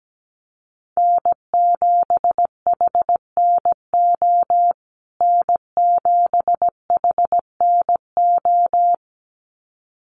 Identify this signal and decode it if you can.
Morse code, heard as a keyed tone, about 17 wpm, 700 Hz. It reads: N7HNO N7HNO
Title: USA Call Signs Repeated 2x 17wpm